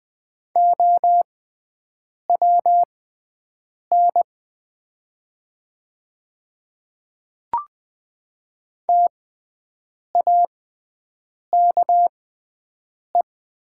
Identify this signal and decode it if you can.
Morse code, heard as a keyed tone, about 20 wpm, spaced as for 7 wpm, 700 Hz. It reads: OWN TAKE
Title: 6x Character Spacing - Top 300 Words - 20wpm